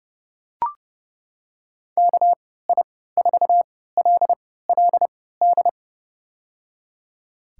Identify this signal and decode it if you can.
Morse code, heard as a keyed tone, about 32 wpm, spaced as for 17 wpm, 700 Hz. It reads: KI4LLD
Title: USA Call Signs - 3x Character Spacing - 30wpm